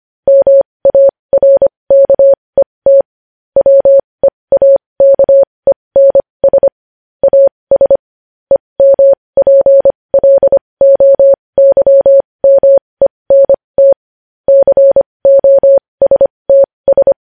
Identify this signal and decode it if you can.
Morse code, heard as a keyed tone, about 25 wpm, spaced as for 20 wpm, 550 Hz. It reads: MARKET WEAKENS AS EMPLOYMENT COSTS
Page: news-headlines-in-morse-code-at-20-wpm